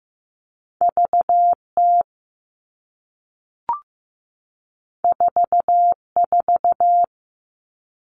Morse code, heard as a keyed tone, about 15 wpm, 700 Hz. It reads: VT 44
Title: All Cut Numbers - 2-Digit 15wpm